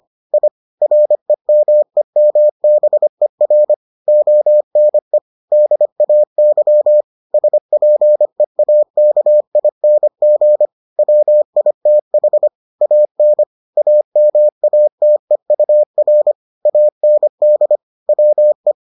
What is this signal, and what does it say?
Morse code, heard as a keyed tone, about 25 wpm, 600 Hz. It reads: I REMEMBER ONE DAY SPEAKING WITH AN AMATEUR AND WE